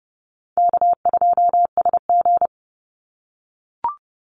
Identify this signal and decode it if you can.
Morse code, heard as a keyed tone, about 30 wpm, 700 Hz. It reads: K2SG
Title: Easy Call Signs - 30wpm